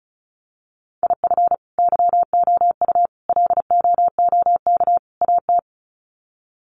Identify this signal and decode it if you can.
Morse code, heard as a keyed tone, about 34 wpm, 700 Hz. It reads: IF YOU LOOK AT
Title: Sets of 4 Words 35wpm